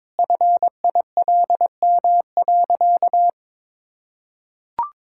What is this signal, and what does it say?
Morse code, heard as a keyed tone, about 21 wpm, 700 Hz. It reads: FILM.
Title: Sentences from Top 1000 Words - Review Part 5 - 22wpm